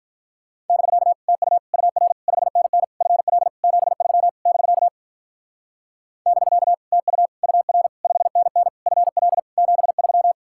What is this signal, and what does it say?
Morse code, heard as a keyed tone, about 43 wpm, 700 Hz. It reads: <BK> TU UR 5NN FL 73 <BK> <BK> TU UR 5NN FL 73
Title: Speed-Racing Common QSO Phrases - 35wpm